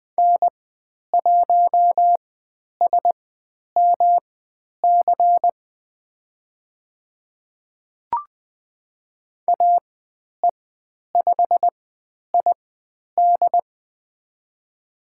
Morse code, heard as a keyed tone, about 20 wpm, spaced as for 10 wpm, 700 Hz. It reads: N1SMC AE5ID
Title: 109 Lesson 18g - Letter m - Call Signs 20wpm